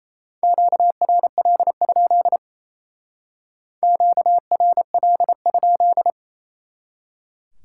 Morse code, heard as a keyed tone, about 37 wpm, 700 Hz. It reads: QRL? QRL?